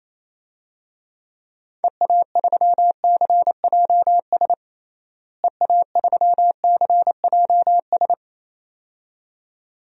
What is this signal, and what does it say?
Morse code, heard as a keyed tone, about 28 wpm, 700 Hz. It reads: EA3CJS EA3CJS
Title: DXCC-100 Call Signs - 6 Characters - Repeated 2x - 28wpm